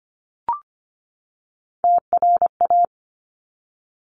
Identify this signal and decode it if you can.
Morse code, heard as a keyed tone, about 25 wpm, 700 Hz. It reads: TRA
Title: Top Three Letter Combinations in Words 25wpm